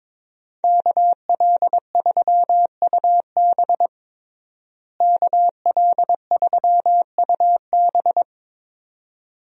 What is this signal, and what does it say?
Morse code, heard as a keyed tone, about 22 wpm, 700 Hz. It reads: KL3UB KL3UB
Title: USA Call Signs - 5 Characters - Repeated 2x - 22wpm